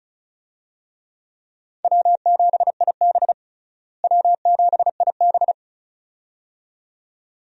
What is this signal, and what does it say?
Morse code, heard as a keyed tone, about 34 wpm, 700 Hz. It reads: W7IB W7IB
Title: USA Call Signs - 4 Characters - Repeated 2x - 35wpm